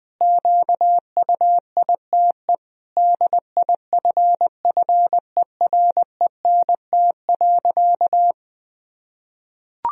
Morse code, heard as a keyed tone, about 21 wpm, 700 Hz. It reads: QUITE DIFFERENT.
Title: Sentences from Top 1000 Words - Review Part 3 - 20wpm